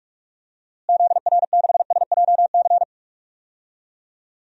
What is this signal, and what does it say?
Morse code, heard as a keyed tone, about 41 wpm, 700 Hz. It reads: ZR6SJC
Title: DXCC-100 Call Signs - 6 Characters - Repeated 3x - 45wpm